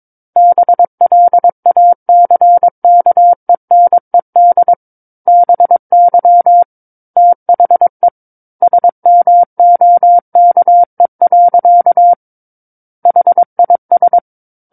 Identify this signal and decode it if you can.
Morse code, heard as a keyed tone, about 22 wpm, 700 Hz. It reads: BLACKENED BY THE SMOKE. HIS
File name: Book One / 13-Chapter 13 - 22 WPM